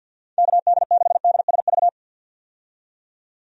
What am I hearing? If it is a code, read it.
Morse code, heard as a keyed tone, about 51 wpm, 700 Hz. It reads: KD6DSV
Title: USA Call Signs - 6 Characters - 50wpm